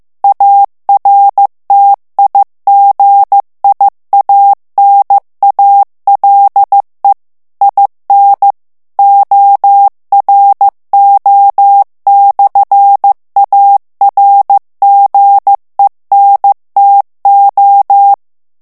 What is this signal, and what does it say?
Morse code, heard as a keyed tone, about 15 wpm, 800 Hz. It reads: ARTIGIANALE IN ORO/ARGENTO
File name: ITALIEN texte clair 1 vitesse 14 mots minute